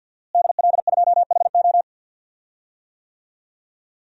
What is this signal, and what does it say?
Morse code, heard as a keyed tone, about 52 wpm, 700 Hz. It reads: DL2HO
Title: DXCC-100 Call Signs - 5 Characters - Repeated 2x - 50wpm